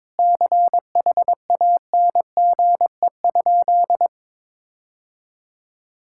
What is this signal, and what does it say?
Morse code, heard as a keyed tone, about 22 wpm, 700 Hz. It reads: CHANGE?
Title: Sentences from Top 1600 Words 22wpm